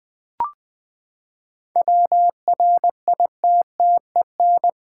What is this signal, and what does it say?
Morse code, heard as a keyed tone, about 20 wpm, 700 Hz. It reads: WRITTEN
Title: Sentences with 2-Word Phrases - 20wpm